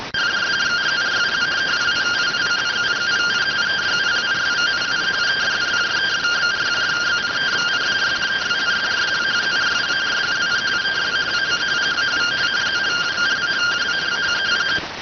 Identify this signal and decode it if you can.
Two-tone teleprinter signal, about 170 Hz shift.